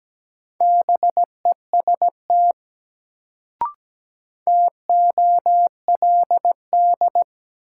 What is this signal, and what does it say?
Morse code, heard as a keyed tone, about 17 wpm, 700 Hz. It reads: BEST TOLD
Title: Sentences from Top 400 Words - New Word Drill - 17wpm